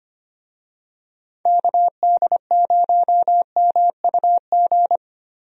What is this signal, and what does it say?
Morse code, heard as a keyed tone, about 25 wpm, 700 Hz. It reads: KD0MUG
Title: USA Call Signs - 6 Characters - Repeated 3x - 25wpm